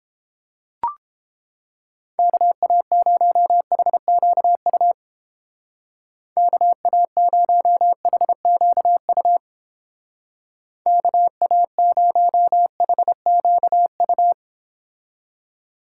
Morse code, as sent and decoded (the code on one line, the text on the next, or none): -.- .- ----- .... --.- ..- / -.- .- ----- .... --.- ..- / -.- .- ----- .... --.- ..-
KA0HQU KA0HQU KA0HQU